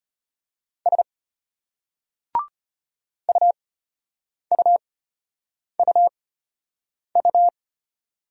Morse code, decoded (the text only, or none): S UUUU